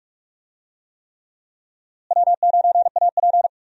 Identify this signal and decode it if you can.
Morse code, heard as a keyed tone, about 42 wpm, 700 Hz. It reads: W9AP
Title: USA Call Signs - 4 Characters - Repeated 3x - 45wpm